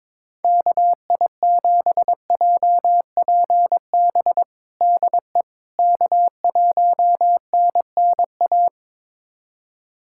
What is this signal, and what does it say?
Morse code, heard as a keyed tone, about 22 wpm, 700 Hz. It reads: KI7JPB DE K1NNA
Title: USA Callsign DE Callsign 22wpm